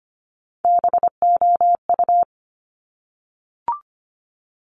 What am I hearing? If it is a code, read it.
Morse code, heard as a keyed tone, about 25 wpm, 700 Hz. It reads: BOU
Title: Top Three Letter Combinations in Words 25wpm